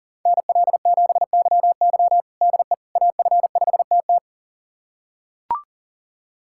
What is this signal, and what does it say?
Morse code, heard as a keyed tone, about 40 wpm, 700 Hz. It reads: NL7YY DE AF5TT
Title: USA Callsign DE Callsign - 5 Characters - 40wpm